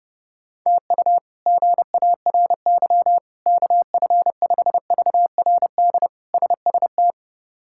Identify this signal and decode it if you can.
Morse code, heard as a keyed tone, about 32 wpm, 700 Hz. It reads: TU GARY KF5VRD SST
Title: K1USN SST QSO Practice - 30wpm